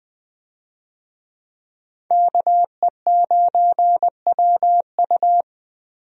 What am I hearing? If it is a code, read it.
Morse code, heard as a keyed tone, about 20 wpm, 700 Hz. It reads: KE9WU
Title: USA Call Signs - 5 Characters - Repeated 2x - 20wpm